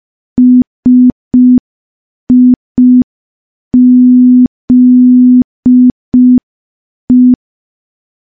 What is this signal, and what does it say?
Morse code, heard as a keyed tone, about 5 wpm, 250 Hz. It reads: SIZE
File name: morse / morse-training-05WPM